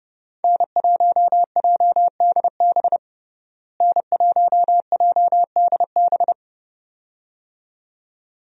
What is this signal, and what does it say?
Morse code, heard as a keyed tone, about 32 wpm, 700 Hz. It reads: N1JDB N1JDB
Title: USA Call Signs - 5 Characters - Repeated 2x - 30wpm